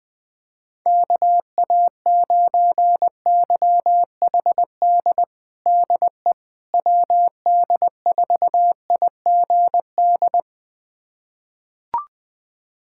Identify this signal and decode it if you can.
Morse code, heard as a keyed tone, about 20 wpm, 700 Hz. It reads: KA9YHD DE WD4IGD
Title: USA Callsign DE Callsign 20wpm